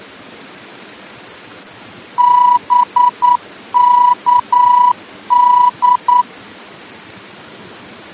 Morse code, heard as a keyed tone, about 9 wpm, 980 Hz. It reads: BKD
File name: germany / mx / BKD-D-VOR